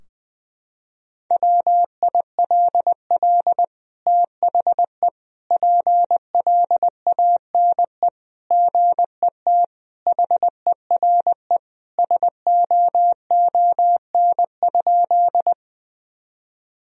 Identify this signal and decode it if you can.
Morse code, heard as a keyed tone, about 20 wpm, 700 Hz. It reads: WILL THE PLANE GET HERE SOON?
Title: Sentences from Top 1000 Words - Review Part 3 - 20wpm